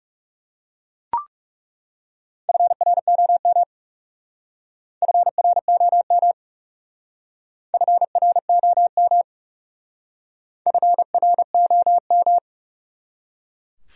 Morse code, heard as a keyed tone, about 36 wpm, 700 Hz. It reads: FROM FROM FROM FROM
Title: Speed-Racing Top 100 Words 30wpm